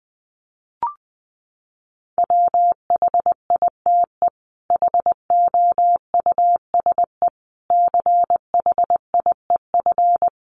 Morse code, heard as a keyed tone, about 20 wpm, 700 Hz. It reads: WHITE HOUSE CHIEF
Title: Sets of 5 Words Encore 20wpm